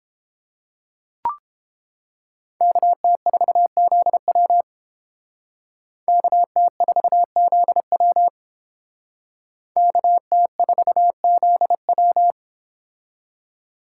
Morse code, heard as a keyed tone, about 29 wpm, 700 Hz. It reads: KT4ZW KT4ZW KT4ZW